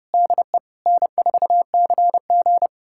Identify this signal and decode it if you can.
Morse code, heard as a keyed tone, about 30 wpm, 700 Hz. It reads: DE N4CG
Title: USA Callsign DE Callsign - 4 Characters - 30wpm